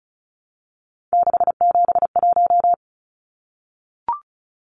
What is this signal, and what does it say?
Morse code, heard as a keyed tone, about 34 wpm, 700 Hz. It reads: 671
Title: Numbers - 3-Digit 35wpm